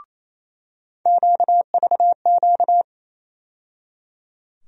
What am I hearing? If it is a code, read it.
Morse code, heard as a keyed tone, about 28 wpm, 700 Hz. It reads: QVQ
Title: USA Call Sign Postfixes - 3 Letters 28wpm